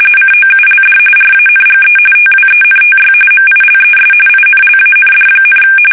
RTTY, about 45.45 baud. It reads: COULD RANGE TWENTY MILES INLAND,
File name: fsk850 / battleshipnj